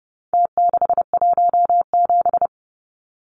Morse code, heard as a keyed tone, about 30 wpm, 700 Hz. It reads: T617
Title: Numbers - Contest Style - 4-Digit 30wpm